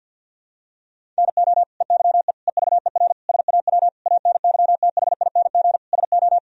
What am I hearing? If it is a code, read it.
Morse code, heard as a keyed tone, about 50 wpm, 700 Hz. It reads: NO EYE EVER SAW ANYTHING SO
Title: Sentences from Top 1000 Words - Review Part 1 - 50wpm